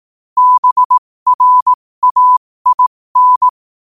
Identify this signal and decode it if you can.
Morse code, heard as a keyed tone, about 18 wpm, spaced as for 15 wpm, 1000 Hz. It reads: BRAIN
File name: qotdfiles / 15wpm